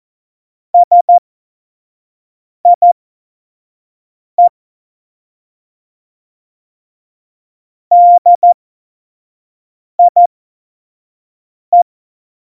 Morse code, heard as a keyed tone, about 14 wpm, spaced as for 5 wpm, 700 Hz. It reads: SIE DIE
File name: Morsekurs / Witze / Witz 30. Januar 2018, 25 F70